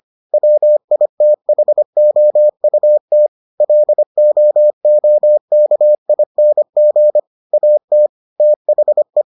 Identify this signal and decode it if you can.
Morse code, heard as a keyed tone, about 25 wpm, 600 Hz. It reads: WITHOUT LOOKING AT THE